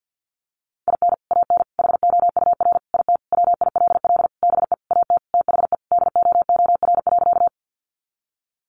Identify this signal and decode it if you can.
Morse code, heard as a keyed tone, about 50 wpm, 700 Hz. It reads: IN AN HOUR IT WILL BE AT THE DOOR.
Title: Sentences from Top 400 Words 50wpm